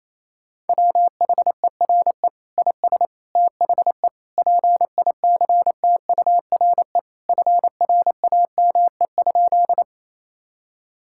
Morse code, heard as a keyed tone, about 28 wpm, 700 Hz. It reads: WHERE IS THE PICTURE FRAME?
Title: Sentences from Top 1000 Words - Review Part 6 - 28wpm